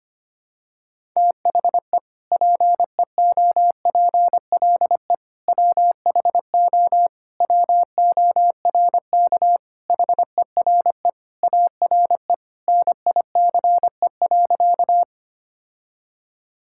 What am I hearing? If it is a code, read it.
Morse code, heard as a keyed tone, about 25 wpm, 700 Hz. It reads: THE PEOPLE WHO WORK HERE ARE NICE.
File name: practice / 25wpm / Sentences with 3-Word Phrases - 25wpm